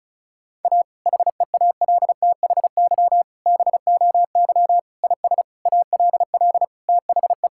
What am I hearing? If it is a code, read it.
Morse code, heard as a keyed tone, about 36 wpm, 700 Hz. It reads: A HEALTHY BOY IS ALL THE
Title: Sentences from Top 1000 Words - Review Part 5 - 35wpm